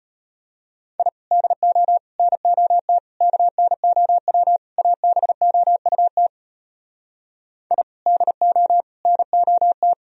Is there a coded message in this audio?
Yes: Morse code from 1.0 to 9.9 s, heard as a keyed tone, about 33 wpm, 700 Hz. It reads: I DO NOT KNOW ABOUT I DO NOT